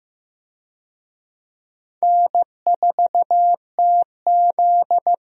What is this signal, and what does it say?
Morse code, heard as a keyed tone, about 15 wpm, 700 Hz. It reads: N4TZ
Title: USA Call Signs - 4 Characters - Repeated 2x - 15wpm